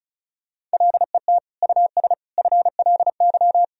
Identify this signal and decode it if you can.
Morse code, heard as a keyed tone, about 36 wpm, 700 Hz. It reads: LET US FLY
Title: Sentences from Top 1000 Words - Review Part 2 - 35wpm